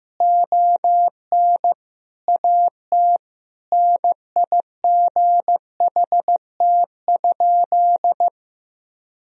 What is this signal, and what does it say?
Morse code, heard as a keyed tone, about 16 wpm, 700 Hz. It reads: ON AT NIGHT?
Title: Sentences from Top 1000 Words - Review Part 4 - 15wpm